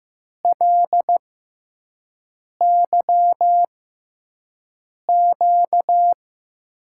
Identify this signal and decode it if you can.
Morse code, heard as a keyed tone, about 15 wpm, spaced as for 5 wpm, 700 Hz. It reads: LYQ
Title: USA Call Signs - 5 Characters - 6x Character Spacing - 15wpm